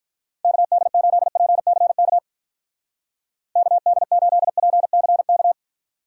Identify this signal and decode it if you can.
Morse code, heard as a keyed tone, about 44 wpm, 700 Hz. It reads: KD8PCK KD8PCK